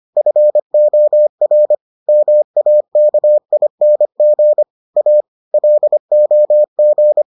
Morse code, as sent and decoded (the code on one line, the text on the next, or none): ..-. --- .-. / -- .- -.- .. -. --. / .- / .-.. --- --.
FOR MAKING A LOG